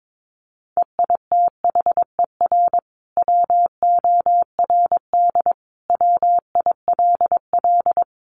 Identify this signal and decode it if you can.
Morse code, heard as a keyed tone, about 22 wpm, 700 Hz. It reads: EITHER WORD WILL
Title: Sentences from Top 600 Words 22wpm